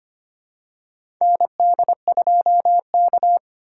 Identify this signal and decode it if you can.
Morse code, heard as a keyed tone, about 25 wpm, 700 Hz. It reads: ND2K
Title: USA Call Signs - 4 Characters - 4x Character Spacing - 25wpm